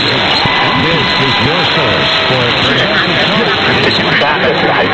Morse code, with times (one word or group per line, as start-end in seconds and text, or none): none